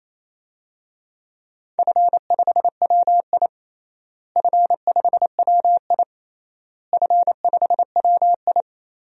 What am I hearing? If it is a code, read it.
Morse code, heard as a keyed tone, about 28 wpm, 700 Hz. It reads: F5WI F5WI F5WI
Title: DXCC-100 Call Signs - 4 Characters - Repeated 3x - 28wpm